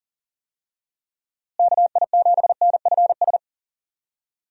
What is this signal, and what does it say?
Morse code, heard as a keyed tone, about 37 wpm, 700 Hz. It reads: KI7NFS
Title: USA Call Signs - 6 Characters - Repeated 2x - 40wpm